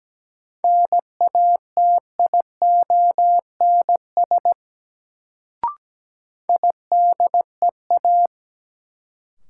Morse code, heard as a keyed tone, about 17 wpm, 700 Hz. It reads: NATIONS IDEA
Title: Sets of 3 Words - New Word Drill - 17wpm